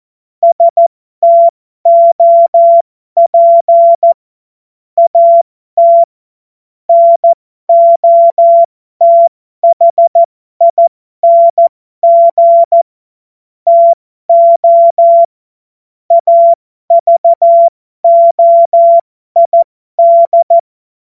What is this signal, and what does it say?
Morse code, heard as a keyed tone, about 14 wpm, spaced as for 12 wpm, 680 Hz. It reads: STOP AT NOTHING TO AVOID
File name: Morsekurs / Witze